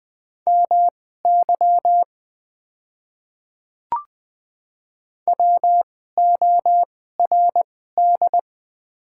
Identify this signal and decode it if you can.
Morse code, heard as a keyed tone, about 20 wpm, spaced as for 14 wpm, 700 Hz. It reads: MY WORD
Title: 2x Character Spacing - Top 100 Words - 20wpm